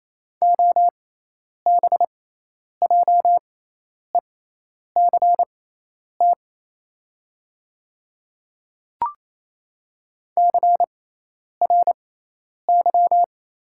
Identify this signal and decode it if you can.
Morse code, heard as a keyed tone, about 28 wpm, spaced as for 10 wpm, 690 Hz. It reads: OBJECT CRY
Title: 6x Character Spacing - Top 500 Words - 28wpm